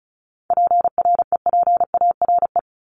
Morse code, heard as a keyed tone, about 34 wpm, 700 Hz. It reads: PREPARE
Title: Top 751-1000 Words 35wpm